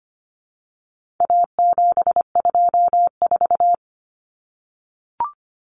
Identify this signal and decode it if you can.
Morse code, heard as a keyed tone, about 25 wpm, 700 Hz. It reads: A724
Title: Common Cut Numbers - 4-Digit 25wpm